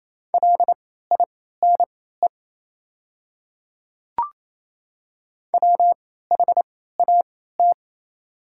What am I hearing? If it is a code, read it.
Morse code, heard as a keyed tone, about 28 wpm, spaced as for 16 wpm, 700 Hz. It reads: LINE WHAT
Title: 3x Character Spacing - Top 200 Words - 28wpm